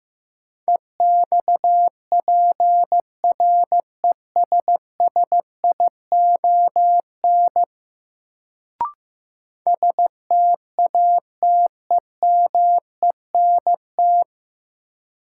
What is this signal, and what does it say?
Morse code, heard as a keyed tone, about 15 wpm, 700 Hz. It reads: EXPRESSION STATEMENT
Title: Sentences from Top 1200 Words - New Word Drill Alt - 15wpm